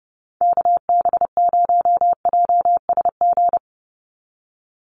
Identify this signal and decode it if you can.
Morse code, heard as a keyed tone, about 30 wpm, 700 Hz. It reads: KB0JSG
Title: USA Call Signs 30wpm